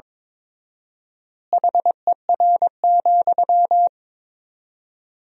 Morse code, heard as a keyed tone, about 22 wpm, 700 Hz. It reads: HER,
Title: 4x Word-Spacing - Sentences from Top 200 Words - 22wpm